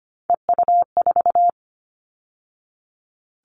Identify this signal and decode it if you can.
Morse code, heard as a keyed tone, about 25 wpm, 700 Hz. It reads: EU4
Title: All Cut Numbers - 3-Digit 25wpm